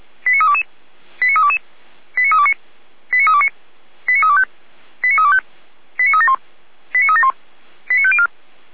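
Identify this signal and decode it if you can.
Five-tone selective call (ZVEI), heard as quick runs of five tones, tones of about 80 ms. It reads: 89322 89322 89329 89329 89325 89325 89371 89371 89504
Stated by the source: ZVEI five-tone selective call received with an SDR.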